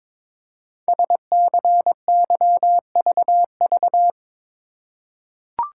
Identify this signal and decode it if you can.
Morse code, heard as a keyed tone, about 22 wpm, 700 Hz. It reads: SCYVV
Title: Code Groups - 5 Letters - 22wpm